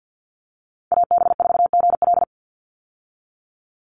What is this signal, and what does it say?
Morse code, heard as a keyed tone, about 50 wpm, 700 Hz. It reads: AB4GL
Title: USA Call Signs 50wpm